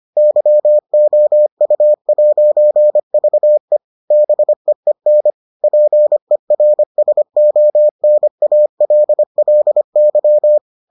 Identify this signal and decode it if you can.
Morse code, heard as a keyed tone, about 25 wpm, 600 Hz. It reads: YOU'VE BEEN PERSONALLY